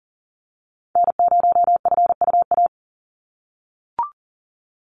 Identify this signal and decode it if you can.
Morse code, heard as a keyed tone, about 40 wpm, 700 Hz. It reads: N0FUA
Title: USA Call Signs 40wpm